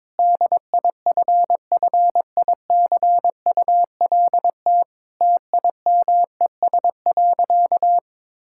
Morse code, heard as a keyed tone, about 22 wpm, 700 Hz. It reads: DIFFICULT TIMES.
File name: Difficult Sentences with Hints - 22wpm